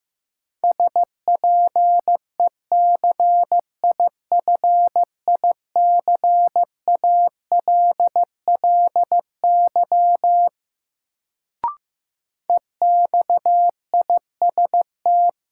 Morse code, heard as a keyed tone, about 15 wpm, 700 Hz. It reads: SPECIFICALLY EXIST
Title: Sentences from Top 1100 Words - New Word Drill Alt - 15wpm